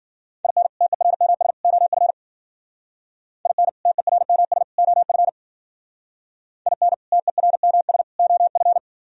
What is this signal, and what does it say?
Morse code, heard as a keyed tone, about 51 wpm, 700 Hz. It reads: IN TERMS OF IN TERMS OF IN TERMS OF